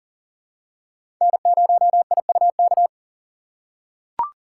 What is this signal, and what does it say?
Morse code, heard as a keyed tone, about 39 wpm, 700 Hz. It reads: N0IUK
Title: USA Call Signs - 5 Characters - 3x Character Spacing - 40wpm